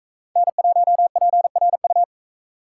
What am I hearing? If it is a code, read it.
Morse code, heard as a keyed tone, about 44 wpm, 700 Hz. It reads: N1PRU